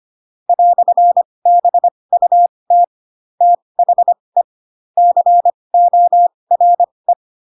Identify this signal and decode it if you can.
Morse code, heard as a keyed tone, about 25 wpm, spaced as for 20 wpm, 700 Hz. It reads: "BUT THE CORE
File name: Pinocchio / Pinocchio ch7 15 25 ik1hge